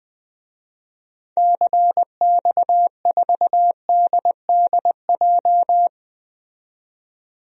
Morse code, heard as a keyed tone, about 21 wpm, 700 Hz. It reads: CX4DDJ
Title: DXCC-100 Call Signs - 6 Characters - 20wpm